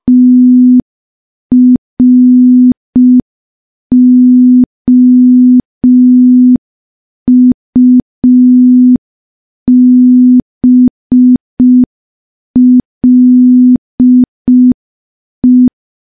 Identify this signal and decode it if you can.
Morse code, heard as a keyed tone, about 5 wpm, 250 Hz. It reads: TROUBLE
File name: morse / morse-training-05WPM